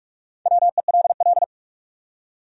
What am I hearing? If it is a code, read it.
Morse code, heard as a keyed tone, about 60 wpm, spaced as for 53 wpm, 700 Hz. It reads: WELL
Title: Speed-Racing Top 200 Words 45wpm